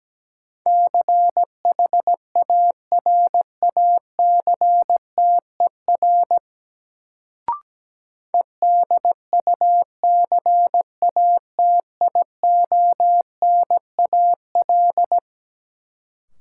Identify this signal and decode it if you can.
Morse code, heard as a keyed tone, about 17 wpm, 700 Hz. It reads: CHARACTER EDUCATIONAL
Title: Sentences from Top 1200 Words - New Word Drill Alt - 17wpm